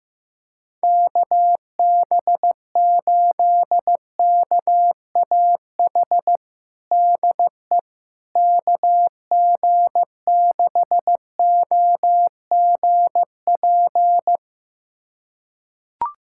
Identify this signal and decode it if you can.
Morse code, heard as a keyed tone, about 16 wpm, 700 Hz. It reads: KB8KAH DE KG6OGP
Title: USA Callsign DE Callsign - 6 Characters - 15wpm